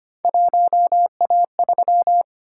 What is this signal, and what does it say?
Morse code, heard as a keyed tone, about 25 wpm, 700 Hz. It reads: 1A3
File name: Three Letter-Number Combinations - 25wpm